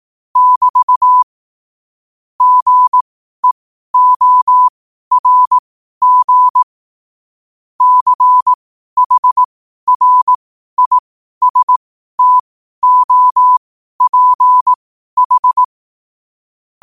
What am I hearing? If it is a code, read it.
Morse code, heard as a keyed tone, about 18 wpm, spaced as for 12 wpm, 1000 Hz. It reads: <BT> GEORG CHRISTOPH
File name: qotdfiles / 12wpm